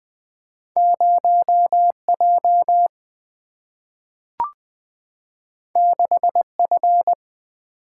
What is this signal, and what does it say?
Morse code, heard as a keyed tone, about 20 wpm, 700 Hz. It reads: 0J 6F
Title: Single-Letter Single-Number 20wpm